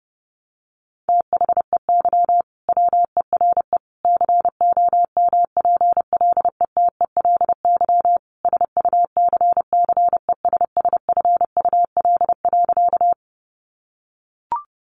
Morse code, heard as a keyed tone, about 30 wpm, 700 Hz. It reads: THEY WERE COMPLETELY SUCCESSFUL.
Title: Sentences from Top 1100 Words 30wpm